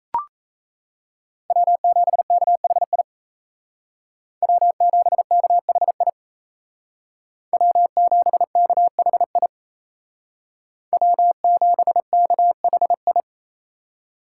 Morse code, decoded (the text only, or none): W7KHI W7KHI W7KHI W7KHI